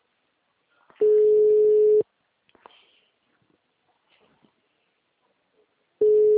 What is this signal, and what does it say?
Ringback tone (Europe, 425 Hz).
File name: out- -2007- - - .12